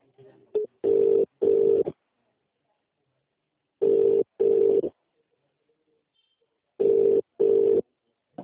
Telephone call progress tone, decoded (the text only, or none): ringback tone